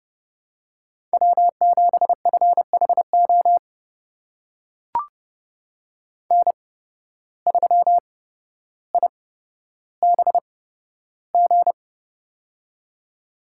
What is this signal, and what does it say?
Morse code, heard as a keyed tone, about 30 wpm, spaced as for 13 wpm, 700 Hz. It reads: W7FHO N3IBG
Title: USA Call Signs - 5 Characters - 8x Character Spacing - 30wpm